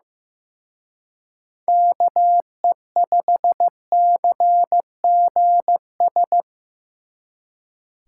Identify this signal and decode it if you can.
Morse code, heard as a keyed tone, about 18 wpm, 700 Hz. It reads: KE5CGS
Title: Speed Racing - USA 2x3 Call Signs 15wpm